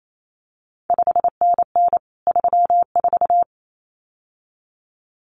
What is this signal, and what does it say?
Morse code, heard as a keyed tone, about 28 wpm, 700 Hz. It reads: 5NN 34
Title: CQ WW DX Contest Exchange Encore - 28wpm